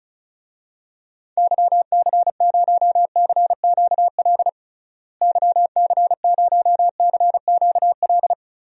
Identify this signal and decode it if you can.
Morse code, heard as a keyed tone, about 34 wpm, 700 Hz. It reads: YC0CQL YC0CQL
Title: DXCC-100 Call Signs - 6 Characters - Repeated 3x - 35wpm